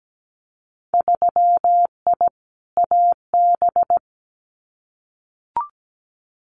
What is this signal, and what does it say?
Morse code, heard as a keyed tone, about 17 wpm, 700 Hz. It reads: 3I AB
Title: Winter Field Day Exchanges Encore 17wpm